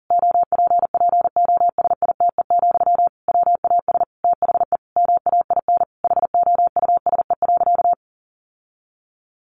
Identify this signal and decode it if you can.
Morse code, heard as a keyed tone, about 40 wpm, 700 Hz. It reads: OPPOSITE, WAS THE MAIN HOUSE.
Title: Sentences from Top 1300 Words 40wpm